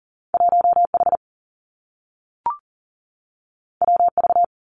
Morse code, heard as a keyed tone, about 42 wpm, 700 Hz. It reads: 1 H W V
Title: Two Challenging Letter-Number Combinations 40wpm